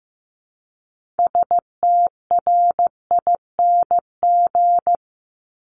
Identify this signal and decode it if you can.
Morse code, heard as a keyed tone, about 15 wpm, 700 Hz. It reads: STRING
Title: Multi-letter Phonemes 15wpm